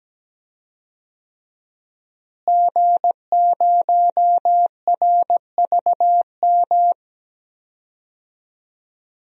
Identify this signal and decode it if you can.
Morse code, heard as a keyed tone, about 17 wpm, 700 Hz. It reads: G0RVM
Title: DXCC-100 Call Signs - 5 Characters - Repeated 3x - 17wpm